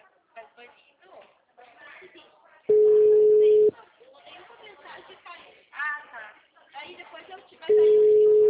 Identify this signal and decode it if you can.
Ringback tone (Europe, 425 Hz), with speech.
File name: out- -2010- - - .19